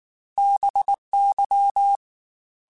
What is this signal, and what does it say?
Morse code, heard as a keyed tone, about 19 wpm, 800 Hz. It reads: BY
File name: Technical Subjects / Technical Subjects-mp3s / XC @18